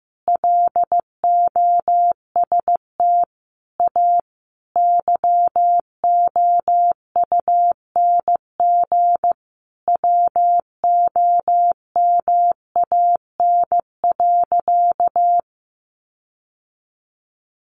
Morse code, heard as a keyed tone, about 15 wpm, 700 Hz. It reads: LOST A YOUNG WOMAN.
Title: Sentences from Top 1400 Words 15wpm